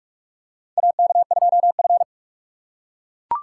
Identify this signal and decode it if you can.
Morse code, heard as a keyed tone, about 42 wpm, 700 Hz. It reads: AK2F
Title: USA Call Signs - 4 Characters - 8x Character Spacing - 45wpm